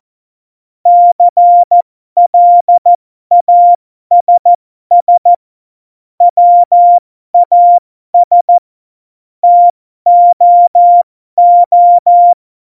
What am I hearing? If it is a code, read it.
Morse code, heard as a keyed tone, about 14 wpm, spaced as for 12 wpm, 700 Hz. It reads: CLASS WAS TOO
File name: Morsekurs / Witze